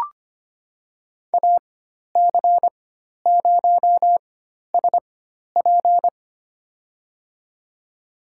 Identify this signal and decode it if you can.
Morse code, heard as a keyed tone, about 25 wpm, spaced as for 12 wpm, 700 Hz. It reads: AC0SP
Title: USA Call Signs - 5 Characters - 4x Character Spacing - 25wpm